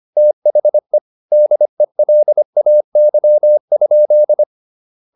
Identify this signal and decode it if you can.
Morse code, heard as a keyed tone, about 25 wpm, 600 Hz. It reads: THE DELAY?